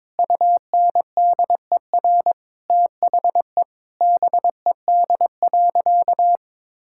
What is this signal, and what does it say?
Morse code, heard as a keyed tone, about 22 wpm, 700 Hz. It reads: UNDER THE BED.
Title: Sentences from Top 1000 Words - Review Part 6 - 22wpm